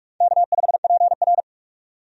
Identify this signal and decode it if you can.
Morse code, heard as a keyed tone, about 46 wpm, 700 Hz. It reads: K5PR
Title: USA Call Signs - 4 Characters - 45wpm